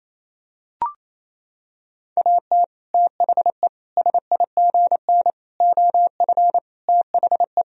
Morse code, heard as a keyed tone, about 28 wpm, 700 Hz. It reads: AT THE SIGN OF THE
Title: Sentences from Top 1000 Words - Review Part 1 - 28wpm